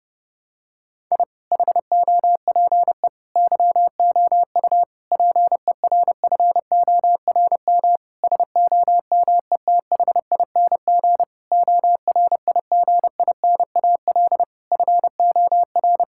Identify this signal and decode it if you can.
Morse code, heard as a keyed tone, about 30 wpm, 700 Hz. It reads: I HOPE YOU PERFORM SOMETHING ORIGINAL FOR
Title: Sentences from Top 1000 Words - Review Part 6 - 30wpm